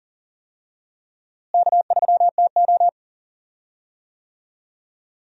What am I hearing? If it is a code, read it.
Morse code, heard as a keyed tone, about 40 wpm, 700 Hz. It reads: K3TO
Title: USA Call Signs - 4 Characters - Repeated 3x - 40wpm